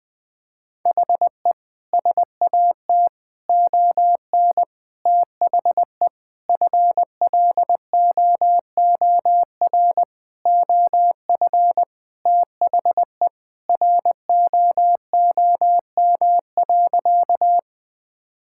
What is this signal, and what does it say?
Morse code, heard as a keyed tone, about 21 wpm, 700 Hz. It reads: HE SAT ON THE FLOOR OF THE ROOM.
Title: Sentences with 5-Word Phrases - 20wpm